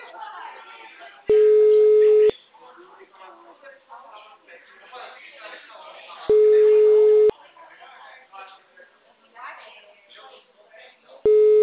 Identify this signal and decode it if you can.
Ringback tone (Europe, 425 Hz), with speech.
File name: out- -2018- - - .86